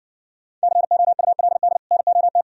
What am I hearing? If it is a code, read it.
Morse code, heard as a keyed tone, about 62 wpm, 700 Hz. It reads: COULD NOT